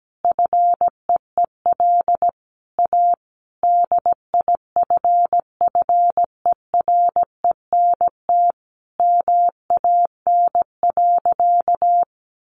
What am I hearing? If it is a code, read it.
Morse code, heard as a keyed tone, about 17 wpm, 700 Hz. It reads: FEEL A DIFFERENT MAN.
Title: Sentences from Top 600 Words 17wpm